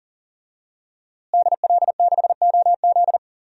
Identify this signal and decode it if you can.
Morse code, heard as a keyed tone, about 40 wpm, 700 Hz. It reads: DL6OZ
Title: DXCC-100 Call Signs - 5 Characters - Repeated 2x - 40wpm